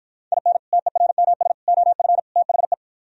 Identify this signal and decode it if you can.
Morse code, heard as a keyed tone, about 52 wpm, 700 Hz. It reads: IN TERMS OF THE